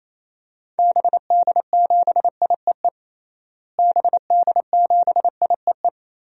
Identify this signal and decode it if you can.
Morse code, heard as a keyed tone, about 28 wpm, 700 Hz. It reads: BD7IEE BD7IEE
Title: DXCC-100 Call Signs - 6 Characters - Repeated 2x - 28wpm